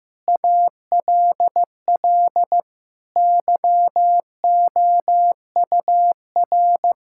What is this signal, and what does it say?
Morse code, heard as a keyed tone, about 15 wpm, 700 Hz. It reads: ALL YOUR
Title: Sentences from Top 1000 Words - Review Part 6 - 15wpm